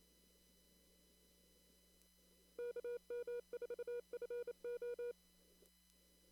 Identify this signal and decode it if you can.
Morse code, heard as a keyed tone, about 28 wpm, 480 Hz. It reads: KM4FO
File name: CW / kt5j wpx cw-op-k5pi